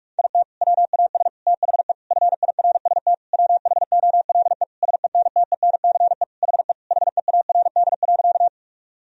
Morse code, heard as a keyed tone, about 46 wpm, 700 Hz. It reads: IT WAS THE FIRST WHOLE SENTENCE HE HEARD.
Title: Sentences from Top 1000 Words - Review Part 1 - 45wpm